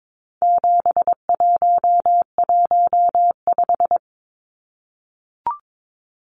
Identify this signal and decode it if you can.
Morse code, heard as a keyed tone, about 22 wpm, 700 Hz. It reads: 7115
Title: Numbers - 4-Digit 22wpm